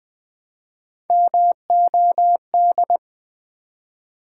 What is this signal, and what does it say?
Morse code, heard as a keyed tone, about 20 wpm, 700 Hz. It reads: MOD
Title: Build A Word - 1 Letter At A Time - 4-Letter Words - 20wpm